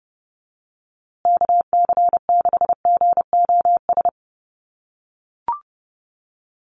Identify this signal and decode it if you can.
Morse code, heard as a keyed tone, about 30 wpm, 700 Hz. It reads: KC6GOS
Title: USA Call Signs Repeated 3x 30wpm